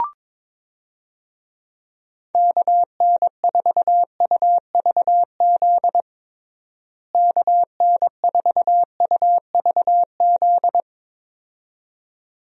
Morse code, heard as a keyed tone, about 22 wpm, 700 Hz. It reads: KN4UVZ KN4UVZ
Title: USA Call Signs - 6 Characters - Repeated 2x - 22wpm